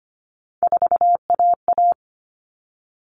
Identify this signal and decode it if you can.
Morse code, heard as a keyed tone, about 25 wpm, 700 Hz. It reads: 4AA
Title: All Cut Numbers - 3-Digit 25wpm